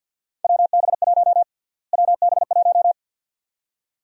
Morse code, heard as a keyed tone, about 53 wpm, 700 Hz. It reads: WB1 WB1
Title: USA Call Sign Prefixes - Repeated 2x - 50wpm